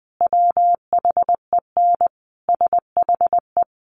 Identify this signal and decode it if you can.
Morse code, heard as a keyed tone, about 20 wpm, 700 Hz. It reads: WHEN SHE
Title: Sentences from Top 1000 Words 20wpm